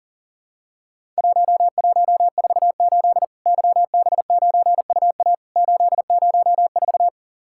Morse code, heard as a keyed tone, about 40 wpm, 700 Hz. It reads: 1148 YB9UA 804
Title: Worked All Europe DX Contest - QTC Exchange - 40wpm